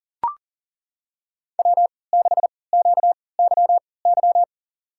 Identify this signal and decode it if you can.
Morse code, heard as a keyed tone, about 38 wpm, spaced as for 22 wpm, 700 Hz. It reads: W6QYY
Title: USA Call Signs - 3x Character Spacing - 40wpm